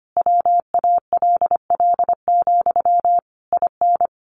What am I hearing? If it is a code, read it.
Morse code, heard as a keyed tone, about 25 wpm, 700 Hz. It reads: WALL, IN
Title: A Sound Of Thunder 25wpm